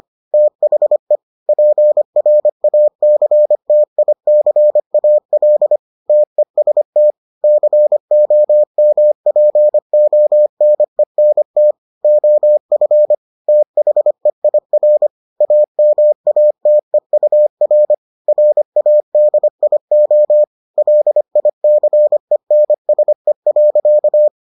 Morse code, heard as a keyed tone, about 25 wpm, 600 Hz. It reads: THE PRACTICAL TEST COMPONENT OF THEIR AMATEUR RADIO LICENSE.